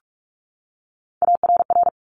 Morse code, heard as a keyed tone, about 46 wpm, 700 Hz. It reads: ARR
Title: Top Three Letter Combinations in Words 45wpm